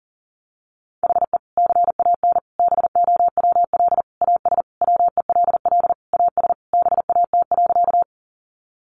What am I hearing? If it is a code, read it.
Morse code, heard as a keyed tone, about 40 wpm, 700 Hz. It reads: HE CAN BOWL AS WELL AS BAT.
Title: Sentences from Top 1500 Words 40wpm